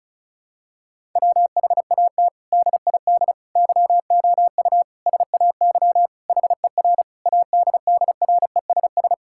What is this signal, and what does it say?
Morse code, heard as a keyed tone, about 36 wpm, 700 Hz. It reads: WHAT DID YOU SAY HER ADDRESS
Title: Sentences from Top 1000 Words - Review Part 6 - 35wpm